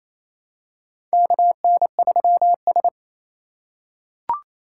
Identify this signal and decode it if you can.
Morse code, heard as a keyed tone, about 28 wpm, 700 Hz. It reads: KN3S
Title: USA Call Signs - 4 Characters - 3x Character Spacing - 28wpm